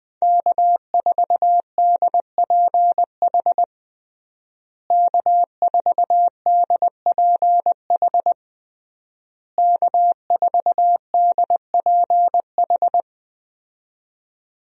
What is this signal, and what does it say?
Morse code, heard as a keyed tone, about 20 wpm, 700 Hz. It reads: K4DPH K4DPH K4DPH
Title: USA Call Signs - 5 Characters - Repeated 3x - 20wpm